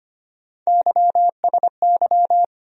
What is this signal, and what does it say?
Morse code, heard as a keyed tone, about 25 wpm, 700 Hz. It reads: YSY
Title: USA Call Sign Postfixes - 3 Letters 25wpm